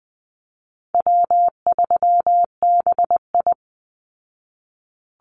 Morse code, heard as a keyed tone, about 20 wpm, 700 Hz. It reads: W3BI
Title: Call Signs Repeated 2x - 20wpm